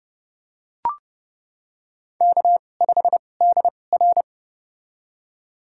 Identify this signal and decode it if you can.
Morse code, heard as a keyed tone, about 32 wpm, spaced as for 22 wpm, 700 Hz. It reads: K5DR
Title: USA Call Signs - 4 Characters - 2x Character Spacing - 30wpm